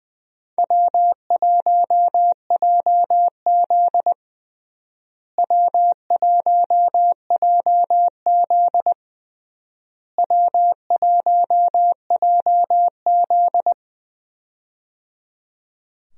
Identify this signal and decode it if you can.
Morse code, heard as a keyed tone, about 20 wpm, 700 Hz. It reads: W1JZ W1JZ W1JZ
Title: USA Call Signs - 4 Characters - Repeated 3x - 20wpm